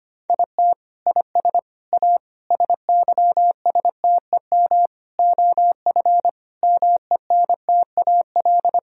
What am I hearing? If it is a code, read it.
Morse code, heard as a keyed tone, about 25 wpm, 700 Hz. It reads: IT IS A SYSTEM OF MENTAL
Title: Sentences from Top 1000 Words - Review Part 4 - 25wpm